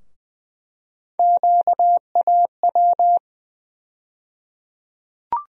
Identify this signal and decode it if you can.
Morse code, heard as a keyed tone, about 20 wpm, 700 Hz. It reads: QAW
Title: USA Call Sign Postfixes - 3 Letters 20wpm